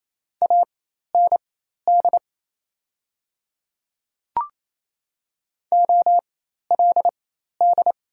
Morse code, heard as a keyed tone, about 28 wpm, spaced as for 13 wpm, 690 Hz. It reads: AND OLD
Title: 4x Character Spacing - Top 200 Words - 28wpm